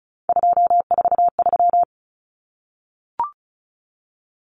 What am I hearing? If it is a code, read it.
Morse code, heard as a keyed tone, about 34 wpm, 700 Hz. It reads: 243
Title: Numbers - 3-Digit 35wpm